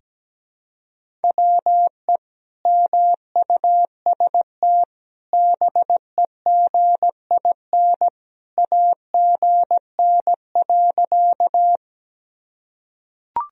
Morse code, heard as a keyed tone, about 17 wpm, 700 Hz. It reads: WE MUST BEGIN AGN.
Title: Sentences from Top 300 Words and Abbreviations 17wpm